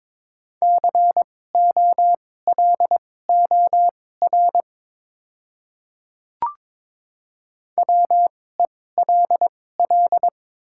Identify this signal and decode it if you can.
Morse code, heard as a keyed tone, about 22 wpm, spaced as for 16 wpm, 700 Hz. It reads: COLOR WELL
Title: 2x Character Spacing - Top 500 Words - 22wpm